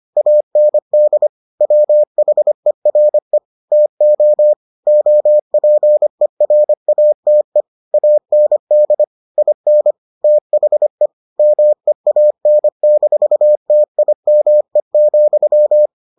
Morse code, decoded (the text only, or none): AND WHERE TO OPERATE AND IN THE MEAN-TIME,